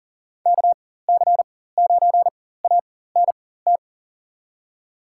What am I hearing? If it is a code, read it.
Morse code, heard as a keyed tone, about 39 wpm, spaced as for 18 wpm, 700 Hz. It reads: KC9ANT
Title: USA Call Signs - 4x Character Spacing - 40wpm